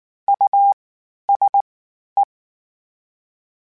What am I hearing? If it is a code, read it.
Morse code, heard as a keyed tone, about 19 wpm, 800 Hz. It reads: U S E
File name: General / General-mp3s / Houses @18